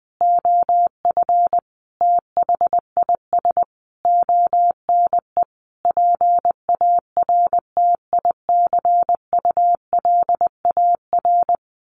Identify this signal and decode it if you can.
Morse code, heard as a keyed tone, about 20 wpm, 700 Hz. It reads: OF THIS ONE PARTICULAR
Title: A Sound Of Thunder 20wpm